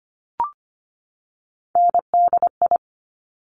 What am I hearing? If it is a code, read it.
Morse code, heard as a keyed tone, about 25 wpm, 700 Hz. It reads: NDI
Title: Top Three Letter Combinations in Words 25wpm